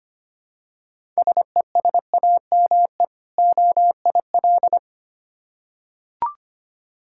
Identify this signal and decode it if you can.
Morse code, heard as a keyed tone, about 25 wpm, 690 Hz. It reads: SESAME OIL
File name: practice / 25wpm / Difficult Words with Hints - 25wpm